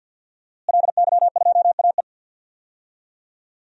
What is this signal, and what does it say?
Morse code, heard as a keyed tone, about 48 wpm, 700 Hz. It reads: LY2AE
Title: DXCC-100 Call Signs - 5 Characters - Repeated 2x - 50wpm